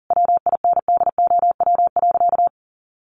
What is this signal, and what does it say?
Morse code, heard as a keyed tone, about 40 wpm, 700 Hz. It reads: WINDOW.
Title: A Sound Of Thunder 40wpm